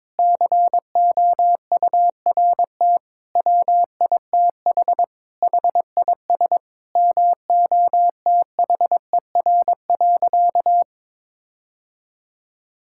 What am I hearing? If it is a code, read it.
Morse code, heard as a keyed tone, about 22 wpm, 700 Hz. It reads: COURT WITH HIS MOTHER.
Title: Sentences from Top 1600 Words 22wpm